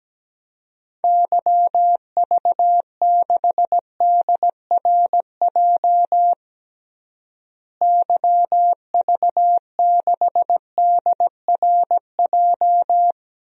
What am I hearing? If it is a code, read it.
Morse code, heard as a keyed tone, about 17 wpm, 700 Hz. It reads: YV6DRJ YV6DRJ
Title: DXCC-100 Call Signs - 6 Characters - Repeated 2x - 17wpm